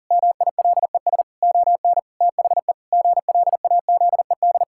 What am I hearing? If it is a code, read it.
Morse code, heard as a keyed tone, about 40 wpm, 700 Hz. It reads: MILES ON THE GLAZED
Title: Pangram Challenge v2 - 40wpm